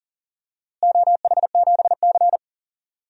Morse code, heard as a keyed tone, about 40 wpm, 700 Hz. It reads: OH7C
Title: DXCC-100 Call Signs - 4 Characters - Repeated 3x - 40wpm